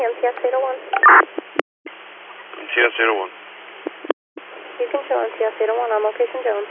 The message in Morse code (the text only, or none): none